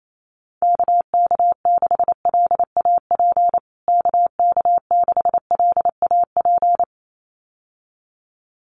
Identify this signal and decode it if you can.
Morse code, heard as a keyed tone, about 28 wpm, 700 Hz. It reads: KK6LAP KK6LAP
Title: USA Call Signs Repeated 2x 28wpm